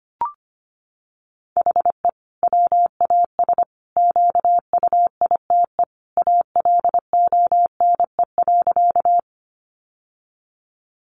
Morse code, heard as a keyed tone, about 25 wpm, 700 Hz. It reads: HE WAS QUITE ALONE.
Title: Sentences from Top 1000 Words 25wpm